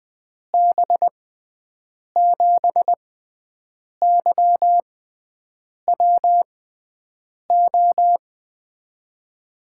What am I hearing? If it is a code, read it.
Morse code, heard as a keyed tone, about 21 wpm, spaced as for 7 wpm, 700 Hz. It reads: B7YWO
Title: USA Call Signs - 6 Characters - 6x Character Spacing - 20wpm